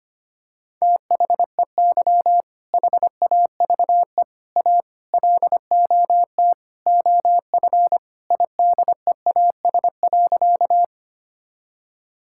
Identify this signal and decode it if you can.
Morse code, heard as a keyed tone, about 25 wpm, 700 Hz. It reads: THEY HAVE A LOT OF IDEAS.
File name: practice / 25wpm / Sentences with 4-Word Phrases - 25wpm